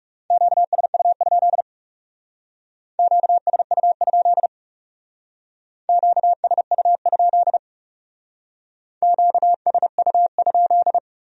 Morse code, heard as a keyed tone, about 38 wpm, 700 Hz. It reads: QSU? QSU? QSU? QSU?